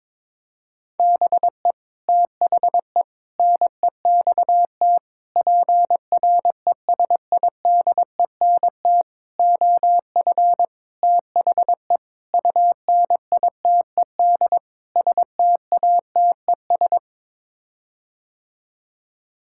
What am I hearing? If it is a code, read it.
Morse code, heard as a keyed tone, about 22 wpm, 700 Hz. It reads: BE THE NEXT PRESIDENT OF THE UNITED STATES
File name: practice / 22wpm / Sets of 8 Words - 22wpm